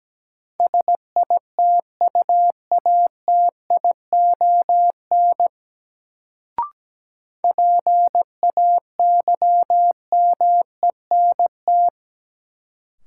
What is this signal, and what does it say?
Morse code, heard as a keyed tone, about 17 wpm, 700 Hz. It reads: SITUATION PAYMENT
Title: Sentences from Top 1200 Words - New Word Drill - 17wpm